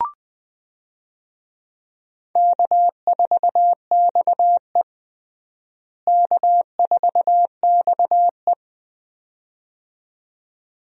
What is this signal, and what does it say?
Morse code, heard as a keyed tone, about 20 wpm, 700 Hz. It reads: K4XE K4XE
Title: USA Call Signs - 4 Characters - Repeated 2x - 20wpm